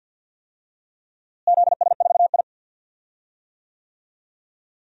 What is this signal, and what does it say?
Morse code, heard as a keyed tone, about 47 wpm, 700 Hz. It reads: ZS4I
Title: DXCC-100 Call Signs - 4 Characters - Repeated 2x - 50wpm